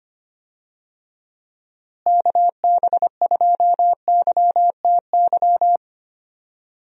Morse code, heard as a keyed tone, about 25 wpm, 700 Hz. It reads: KB2YTY
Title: USA Call Signs - 6 Characters - Repeated 3x - 25wpm